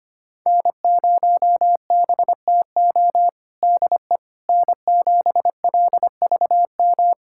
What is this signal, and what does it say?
Morse code, heard as a keyed tone, about 25 wpm, 700 Hz. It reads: N0BTO DE N7LVM
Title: USA Callsign DE Callsign - 5 Characters - 25wpm